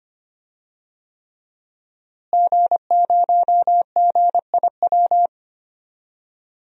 Morse code, heard as a keyed tone, about 25 wpm, 700 Hz. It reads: G0GIW
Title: DXCC-100 Call Signs - 5 Characters - Repeated 3x - 25wpm